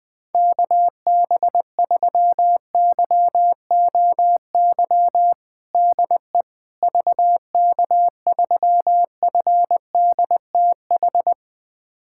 Morse code, heard as a keyed tone, about 20 wpm, 700 Hz. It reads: KB3YOY DE VK3FDTH
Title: USA Callsign DE Callsign 20wpm